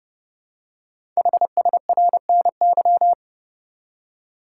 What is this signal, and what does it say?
Morse code, heard as a keyed tone, about 28 wpm, 700 Hz. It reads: HSRNY
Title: Code Groups - 5 Letters - 30wpm